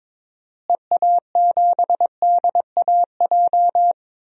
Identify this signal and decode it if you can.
Morse code, heard as a keyed tone, about 22 wpm, 700 Hz. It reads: EA7DAJ
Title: DXCC-100 Call Signs - 6 Characters - Repeated 2x - 22wpm